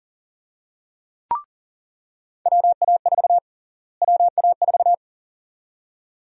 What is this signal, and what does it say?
Morse code, heard as a keyed tone, about 40 wpm, 700 Hz. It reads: WA4 WA4
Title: USA Call Sign Prefixes - Repeated 2x - 40wpm